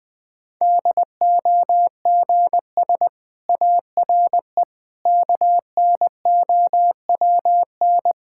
Morse code, heard as a keyed tone, about 20 wpm, 700 Hz. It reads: DOGS ARE KNOWN
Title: Difficult Sentences with Hints - 20wpm